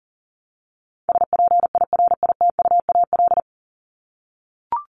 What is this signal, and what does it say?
Morse code, heard as a keyed tone, about 40 wpm, 700 Hz. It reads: SPIRITUAL
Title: Top 1501-1750 Words 40wpm